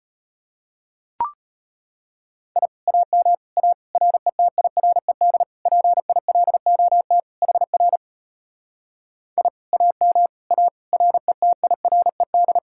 Morse code, heard as a keyed tone, about 33 wpm, 700 Hz. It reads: I AM A RETIRED PILOT HR I AM A RETIRED